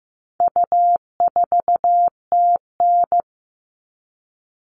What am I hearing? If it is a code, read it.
Morse code, heard as a keyed tone, about 15 wpm, 700 Hz. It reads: U4TN
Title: All Cut Numbers - 4-Digit 15wpm